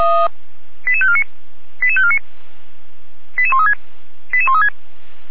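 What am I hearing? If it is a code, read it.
Five-tone selective call (ZVEI), heard as quick runs of five tones, tones of about 70 ms. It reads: 88549 88549 88136 88136
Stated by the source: ZVEI five-tone call received by SDR, from the audio archive of Feuerwehr Orlamünde.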